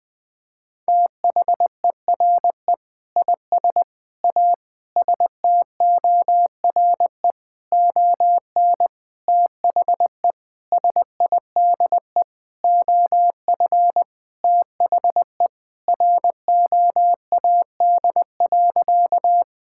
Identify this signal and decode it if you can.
Morse code, heard as a keyed tone, about 20 wpm, 700 Hz. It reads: THERE IS A STORE ON THE SIDE OF THE ROAD.
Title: Sentences with 6-Word Phrases - Spoken Beginning - 20wpm